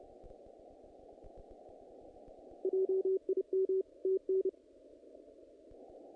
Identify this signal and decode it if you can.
Morse code, heard as a keyed tone, about 30 wpm, 380 Hz. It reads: JIM TN
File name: contests / naqp-cw-k5tr-op-k5pi- -1900